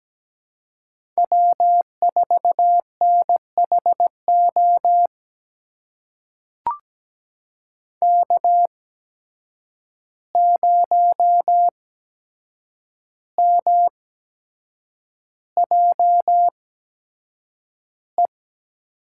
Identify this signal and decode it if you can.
Morse code, heard as a keyed tone, about 17 wpm, spaced as for 7 wpm, 700 Hz. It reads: W4NHO K0MJE
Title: USA Call Signs - 5 Characters - 8x Character Spacing - 17wpm